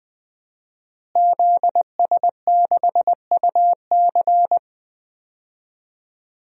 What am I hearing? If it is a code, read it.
Morse code, heard as a keyed tone, about 20 wpm, 700 Hz. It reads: ZS6UC
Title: DXCC-100 Call Signs - 5 Characters - Repeated 2x - 20wpm